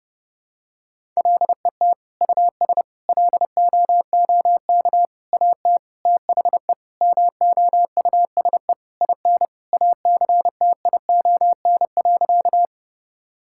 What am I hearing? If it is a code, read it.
Morse code, heard as a keyed tone, about 32 wpm, 700 Hz. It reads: LET US LOOK AT THE MOUSE IN ACTION.
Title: Sentences from Top 1000 Words - Review Part 5 - 30wpm